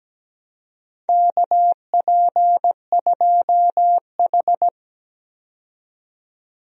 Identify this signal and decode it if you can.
Morse code, heard as a keyed tone, about 17 wpm, 700 Hz. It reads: KP2H
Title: DXCC-100 Call Signs - 4 Characters - 17wpm